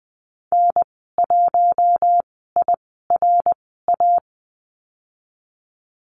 Morse code, heard as a keyed tone, about 20 wpm, spaced as for 15 wpm, 700 Hz. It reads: N1IRA
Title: 088 Lesson 15g - Number 5 - Call Signs 20wpm